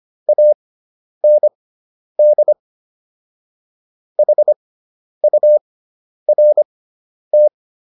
Morse code, heard as a keyed tone, about 25 wpm, spaced as for 10 wpm, 600 Hz. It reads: AND HURT